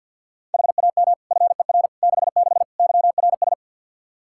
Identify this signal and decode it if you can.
Morse code, heard as a keyed tone, about 52 wpm, 700 Hz. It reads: HAM FER 66 YRS
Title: Practice QSO - Exchange 3 - More Info - 50wpm